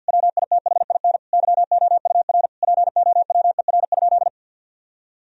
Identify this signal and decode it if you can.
Morse code, heard as a keyed tone, about 50 wpm, 700 Hz. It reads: WITHIN YOUR POWER?
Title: Sentences from Top 1000 Words - Review Part 2 - 50wpm